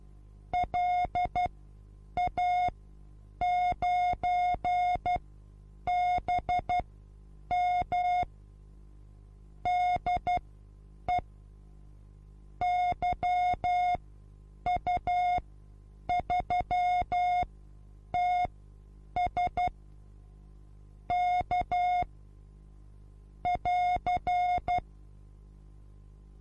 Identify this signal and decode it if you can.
Morse code, heard as a keyed tone, about 12 wpm, spaced as for 8 wpm, 720 Hz. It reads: LA9BM DE YU3TS K <AR>
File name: repos / Morsekurs / DJ4UF Morsen23